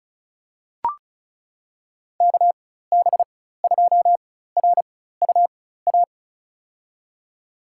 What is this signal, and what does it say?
Morse code, heard as a keyed tone, about 36 wpm, spaced as for 16 wpm, 700 Hz. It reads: KB2RUA
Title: USA Call Signs - 6 Characters - 4x Character Spacing - 35wpm